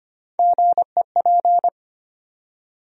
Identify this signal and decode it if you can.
Morse code, heard as a keyed tone, about 25 wpm, 700 Hz. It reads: GEP
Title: USA Call Sign Postfixes - 3 Letters 25wpm